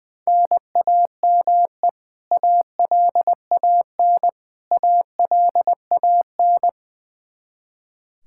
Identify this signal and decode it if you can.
Morse code, heard as a keyed tone, about 20 wpm, 700 Hz. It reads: NAME ALAN ALAN
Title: Practice QSO - Exchange 1 - Beginning - 20wpm